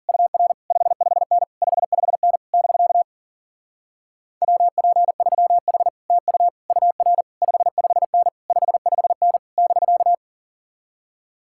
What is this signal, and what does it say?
Morse code, heard as a keyed tone, about 53 wpm, 700 Hz. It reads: UR 55N 55N <BK> WP3H TU UR 55N 55N <BK>